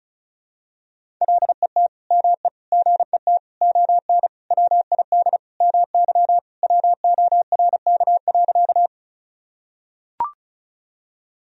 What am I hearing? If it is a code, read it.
Morse code, heard as a keyed tone, about 36 wpm, 700 Hz. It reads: LET ME GET ON WID MY WORK.
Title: Sentences from Top 300 Words and Abbreviations 35wpm